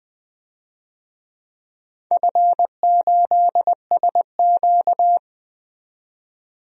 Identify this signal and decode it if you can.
Morse code, heard as a keyed tone, about 20 wpm, 700 Hz. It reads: F8SQ
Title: DXCC-100 Call Signs - 4 Characters - 20wpm